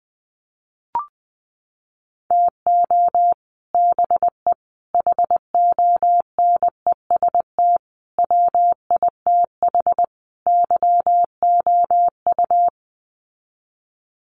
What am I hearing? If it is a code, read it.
Morse code, heard as a keyed tone, about 20 wpm, 700 Hz. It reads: TO BE HONEST WITH YOU
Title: Sets of 5 Words Encore 20wpm